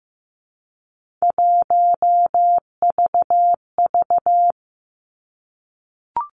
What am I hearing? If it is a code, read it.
Morse code, heard as a keyed tone, about 15 wpm, 700 Hz. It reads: 1VV
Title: Three Challenging Letter-Number Combinations 15wpm